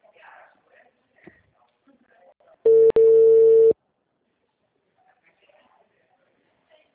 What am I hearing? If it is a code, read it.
Ringback tone (China, 450 Hz), with speech.